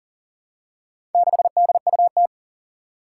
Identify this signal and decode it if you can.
Morse code, heard as a keyed tone, about 40 wpm, 700 Hz. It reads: 6DUT
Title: Four Challenging Letter-Number Combinations 40wpm